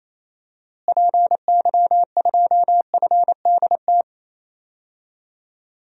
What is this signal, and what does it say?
Morse code, heard as a keyed tone, about 28 wpm, 700 Hz. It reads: PY2FDT
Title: DXCC-100 Call Signs - 6 Characters - Repeated 2x - 28wpm